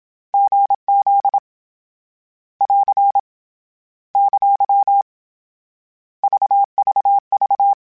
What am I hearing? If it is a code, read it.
Morse code, heard as a keyed tone, about 27 wpm, 800 Hz. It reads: GZ <AR> ! VVV
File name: CW Code Groups-mp3s / Sets mp3s / GZ GZ7 sp @25